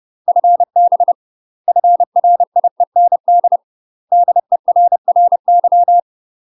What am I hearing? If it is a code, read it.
Morse code, heard as a keyed tone, about 30 wpm, 700 Hz. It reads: FB FRIEND DERRY
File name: UK MASTER QSOs TRAINING / 30WPM / UK MASTER QSOs 0042